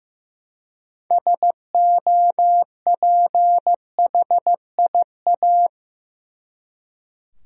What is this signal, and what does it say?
Morse code, heard as a keyed tone, about 15 wpm, 700 Hz. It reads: SOPHIA
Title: USA Common Names - 15wpm